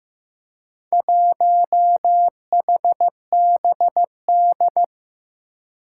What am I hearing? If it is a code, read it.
Morse code, heard as a keyed tone, about 15 wpm, 700 Hz. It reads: 1HBD
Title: Four Challenging Letter-Number Combinations 15wpm